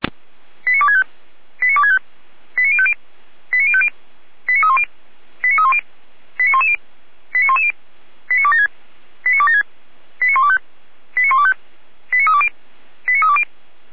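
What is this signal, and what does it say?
Five-tone selective call (ZVEI), heard as quick runs of five tones, tones of about 70 ms. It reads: 89265 89265 89050 89050 89310 89310 89110 89110 89276 89276 89125 89125 89320 89320
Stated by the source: ZVEI five-tone selective call 89073 received by SDR, from the feuerwehr-orlamuende.org (Orlamünde fire brigade) audio server.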